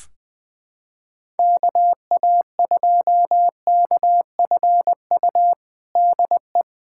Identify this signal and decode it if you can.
Morse code, heard as a keyed tone, about 21 wpm, 700 Hz. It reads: KA2KFU DE
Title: USA Callsign DE Callsign - 6 Characters - 20wpm